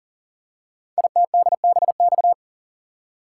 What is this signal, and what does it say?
Morse code, heard as a keyed tone, about 40 wpm, 700 Hz. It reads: ITDBX
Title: Code Groups - 5 Letters - 40wpm